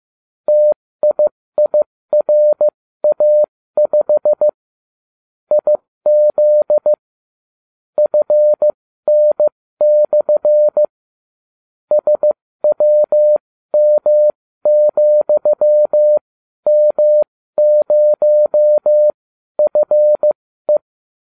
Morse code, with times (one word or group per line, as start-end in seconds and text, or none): 0.5-4.5 TIIRA5
5.5-6.9 IZ
8.0-10.8 FN/
11.9-20.8 SWM,M0FE